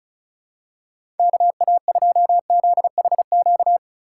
Morse code, heard as a keyed tone, about 36 wpm, 700 Hz. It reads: KA2ZHQ
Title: USA Call Signs - 2x Character Spacing - 35wpm